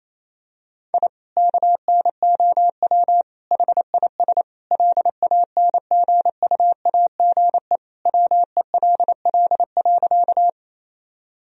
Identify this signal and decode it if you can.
Morse code, heard as a keyed tone, about 28 wpm, 700 Hz. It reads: I KNOW HIS LANGUAGE WELL.
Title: Sentences from Top 1000 Words - Review Part 3 - 28wpm